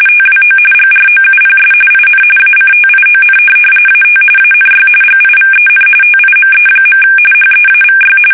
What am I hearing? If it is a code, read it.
RTTY, about 45.45 baud. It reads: TWO GROUPS IN ACTION AT ANY GIVEN TIME. HEAVY
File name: fsk850 / USS-Massachusetts